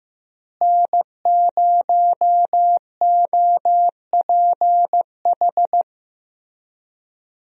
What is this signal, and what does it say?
Morse code, heard as a keyed tone, about 16 wpm, 700 Hz. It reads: N0OPH
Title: USA Call Signs - 5 Characters - Repeated 3x - 15wpm